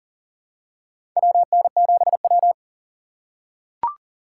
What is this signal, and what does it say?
Morse code, heard as a keyed tone, about 40 wpm, 700 Hz. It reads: WN7W
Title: USA Call Signs - 4 Characters - 3x Character Spacing - 40wpm